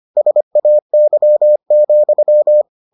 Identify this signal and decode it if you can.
Morse code, heard as a keyed tone, about 25 wpm, 600 Hz. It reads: SAY,